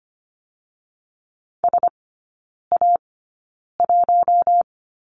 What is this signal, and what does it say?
Morse code, heard as a keyed tone, about 25 wpm, spaced as for 9 wpm, 690 Hz. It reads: SA1
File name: Lesson 14e - Number 2 - Three Characters 25wpm